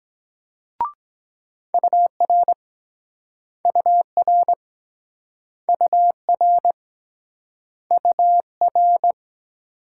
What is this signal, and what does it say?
Morse code, heard as a keyed tone, about 24 wpm, 700 Hz. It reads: UR UR UR UR